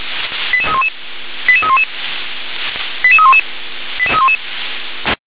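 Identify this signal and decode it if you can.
Five-tone selective call (ZVEI), heard as quick runs of five tones, tones of about 70 ms. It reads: 88311 88311 88311 88311
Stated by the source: Five-tone ZVEI selective call received on SDR.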